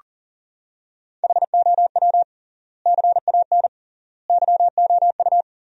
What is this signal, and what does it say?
Morse code, heard as a keyed tone, about 40 wpm, 700 Hz. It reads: HOW CAN YOU
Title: 3x Word-Spacing - Sentences from Top 200 Words - 40wpm